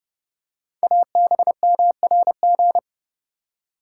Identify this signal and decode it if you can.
Morse code, heard as a keyed tone, about 30 wpm, 700 Hz. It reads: ABMRG
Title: Code Groups - 5 Letters - 30wpm